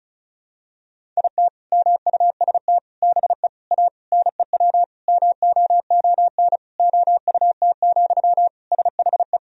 Morse code, heard as a keyed tone, about 36 wpm, 700 Hz. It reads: IT MUST BE A NEW MOON OUT, SHE
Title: Sentences from Top 500 Words and Abbreviations 35wpm